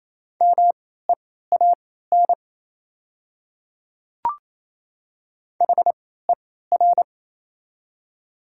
Morse code, heard as a keyed tone, about 28 wpm, spaced as for 16 wpm, 700 Hz. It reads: MEAN HER
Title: 3x Character Spacing - Top 200 Words - 28wpm